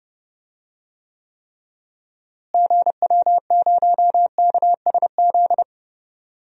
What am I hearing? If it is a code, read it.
Morse code, heard as a keyed tone, about 31 wpm, 700 Hz. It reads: GW0KSZ
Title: DXCC-100 Call Signs - 6 Characters - Repeated 3x - 30wpm